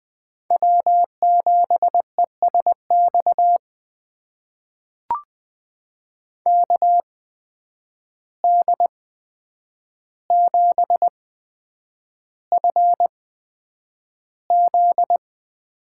Morse code, heard as a keyed tone, about 20 wpm, spaced as for 9 wpm, 700 Hz. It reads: W7ESX KD7FZ
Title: USA Call Signs - 5 Characters - 8x Character Spacing - 20wpm